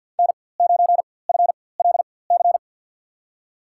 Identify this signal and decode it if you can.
Morse code, heard as a keyed tone, about 50 wpm, spaced as for 23 wpm, 700 Hz. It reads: N9FLC
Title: USA Call Signs - 5 Characters - 4x Character Spacing - 50wpm